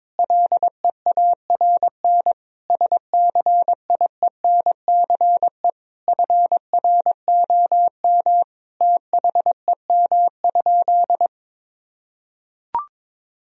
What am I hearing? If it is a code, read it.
Morse code, heard as a keyed tone, about 22 wpm, 700 Hz. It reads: LEARN SCIENCE FROM THEM?
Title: Sentences from Top 1000 Words - Review Part 2 - 22wpm